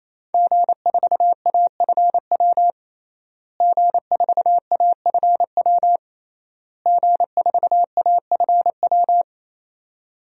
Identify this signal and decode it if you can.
Morse code, heard as a keyed tone, about 28 wpm, 700 Hz. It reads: G4AFW G4AFW G4AFW
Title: DXCC-100 Call Signs - 5 Characters - Repeated 3x - 28wpm